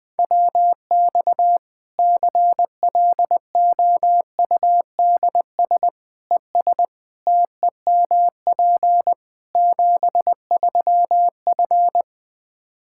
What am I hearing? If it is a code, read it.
Morse code, heard as a keyed tone, about 20 wpm, 700 Hz. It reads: WX CLOUDS ES TEMP 73F
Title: Practice QSO - Exchange 2 - Info - 20wpm